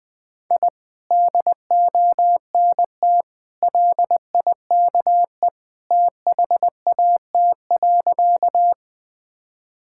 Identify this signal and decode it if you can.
Morse code, heard as a keyed tone, about 20 wpm, 700 Hz. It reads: I DONT LIKE THAT.
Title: Sentences with 2-Word Phrases - Spoken Beginning - 20wpm